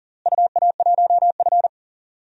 Morse code, heard as a keyed tone, about 40 wpm, 700 Hz. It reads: UA1F
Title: DXCC-100 Call Signs - 4 Characters - Repeated 2x - 40wpm